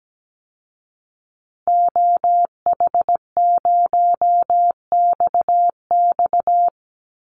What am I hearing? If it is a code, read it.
Morse code, heard as a keyed tone, about 17 wpm, 700 Hz. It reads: OH0XX
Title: Call Signs Repeated 3x - 17wpm